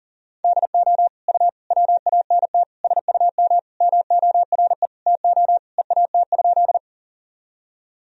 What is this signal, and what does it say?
Morse code, heard as a keyed tone, about 40 wpm, 700 Hz. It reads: DO U WANT SUM MORE TO EAT?
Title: Sentences from Top 300 Words and Abbreviations 40wpm